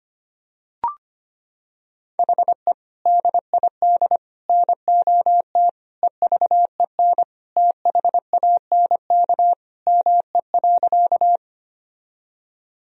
Morse code, heard as a keyed tone, about 25 wpm, 700 Hz. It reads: HE DID NOT EVEN THANK ME.
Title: Sentences from Top 1000 Words - Review Part 1 - 25wpm